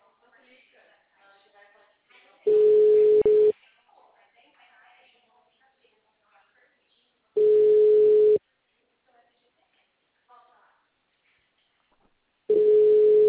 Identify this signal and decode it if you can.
Ringback tone (Europe, 425 Hz), with speech.
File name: out- -2013- - - .6567